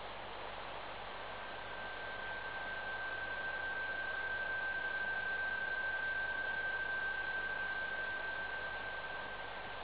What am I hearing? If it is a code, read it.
PSK31, about 31.25 baud.